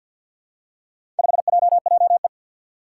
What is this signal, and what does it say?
Morse code, heard as a keyed tone, about 50 wpm, 700 Hz. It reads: 5JJE
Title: Four Challenging Letter-Number Combinations 50wpm